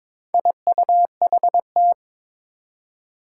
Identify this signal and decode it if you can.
Morse code, heard as a keyed tone, about 23 wpm, 700 Hz. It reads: IUHT
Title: Four Challenging Letter-Number Combinations 22wpm